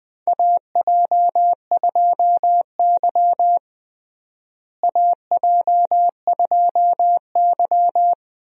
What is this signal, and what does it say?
Morse code, heard as a keyed tone, about 20 wpm, 700 Hz. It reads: AJ2Y AJ2Y
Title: USA Call Signs - 4 Characters - Repeated 3x - 20wpm